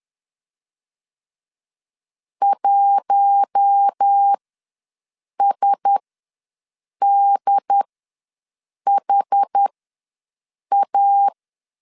Morse code, heard as a keyed tone, about 10 wpm, spaced as for 6 wpm, 800 Hz. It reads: 1SDHA
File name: LEZIONI / 5 LBVC / 5 LBVC ws 10wpm 3733